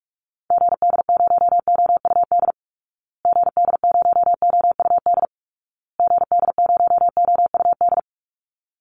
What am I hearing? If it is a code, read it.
Morse code, heard as a keyed tone, about 44 wpm, 700 Hz. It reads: GD0OUD GD0OUD GD0OUD
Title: Call Signs Repeated 2x - 45wpm